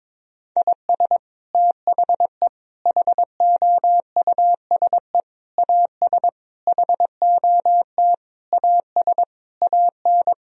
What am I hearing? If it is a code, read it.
Morse code, heard as a keyed tone, about 22 wpm, 700 Hz. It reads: IS THE HOUSE AS HOT AS AN
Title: Sentences from Top 1000 Words - Review Part 4 - 22wpm